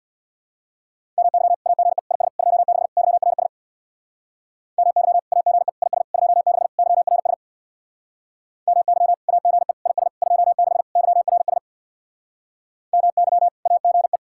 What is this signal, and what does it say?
Morse code, heard as a keyed tone, about 62 wpm, 700 Hz. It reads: MY AGE IS 26 YRS MY AGE IS 26 YRS MY AGE IS 26 YRS MY AGE
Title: Speed-Racing Common QSO Phrases - 50wpm